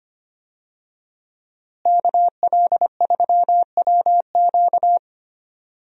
Morse code, heard as a keyed tone, about 25 wpm, 700 Hz. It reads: KL3WQ
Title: USA Call Signs - 5 Characters - Repeated 3x - 25wpm